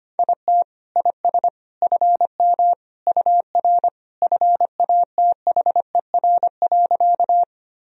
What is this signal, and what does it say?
Morse code, heard as a keyed tone, about 25 wpm, 700 Hz. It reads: IT IS FM UR FATHER.
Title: Sentences from Top 300 Words and Abbreviations 25wpm